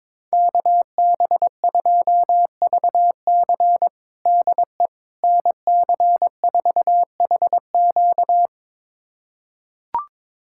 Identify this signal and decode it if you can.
Morse code, heard as a keyed tone, about 22 wpm, 700 Hz. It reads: KB2VC DE NC4HQ
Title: USA Callsign DE Callsign - 5 Characters - 22wpm